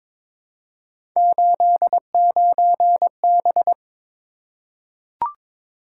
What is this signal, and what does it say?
Morse code, heard as a keyed tone, about 22 wpm, 700 Hz. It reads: 89B
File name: practice / 22wpm / Three Letter-Number Combinations - 22wpm